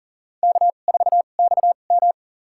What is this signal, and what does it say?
Morse code, heard as a keyed tone, about 40 wpm, spaced as for 29 wpm, 700 Hz. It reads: K4XM
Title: USA Call Signs - 4 Characters - 2x Character Spacing - 40wpm